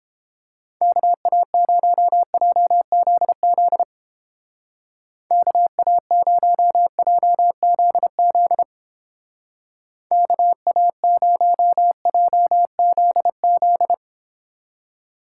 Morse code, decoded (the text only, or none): KA0JZZ KA0JZZ KA0JZZ